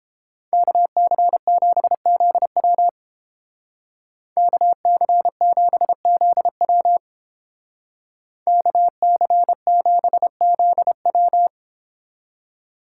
Morse code, heard as a keyed tone, about 30 wpm, 700 Hz. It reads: KC7ZW KC7ZW KC7ZW